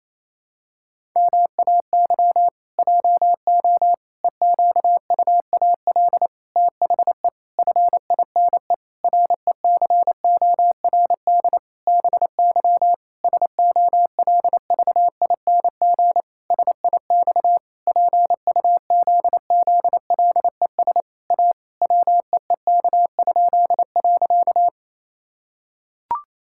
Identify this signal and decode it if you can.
Morse code, heard as a keyed tone, about 28 wpm, 700 Hz. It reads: MAY JO EQUAL THE FINE RECORD BY SOLVING SIX PUZZLES A WEEK?.
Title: Pangram Challenge v2 - 28wpm